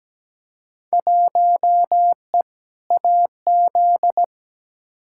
Morse code, heard as a keyed tone, about 17 wpm, 700 Hz. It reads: 1E AZ
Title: Field Day Exchanges - 17wpm